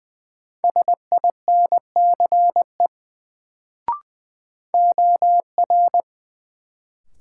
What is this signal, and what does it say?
Morse code, heard as a keyed tone, about 20 wpm, 700 Hz. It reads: SINCE OR
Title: Sets of 2 Words - New Word Drill - 20wpm